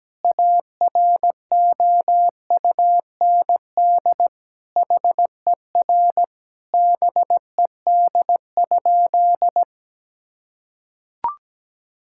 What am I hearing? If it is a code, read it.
Morse code, heard as a keyed tone, about 17 wpm, 700 Hz. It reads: AROUND HER BED?
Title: Sentences from Top 1000 Words - Review Part 6 - 17wpm